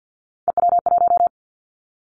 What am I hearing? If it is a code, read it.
Morse code, heard as a keyed tone, about 50 wpm, 700 Hz. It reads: EW1
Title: Callsign Prefixes for Top 100 DXCC - 50wpm